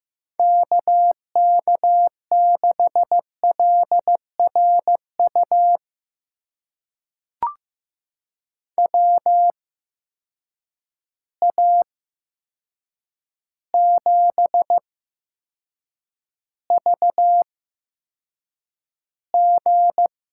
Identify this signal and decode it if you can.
Morse code, heard as a keyed tone, about 15 wpm, spaced as for 7 wpm, 700 Hz. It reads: KK6LRU WA7VG
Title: USA Call Signs - 6 Characters - 8x Character Spacing - 15wpm